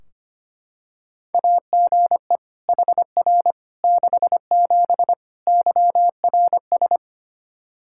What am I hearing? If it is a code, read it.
Morse code, heard as a keyed tone, about 25 wpm, 700 Hz. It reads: AGE HR 67 YRS
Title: K1USN SST QSO Practice 25wpm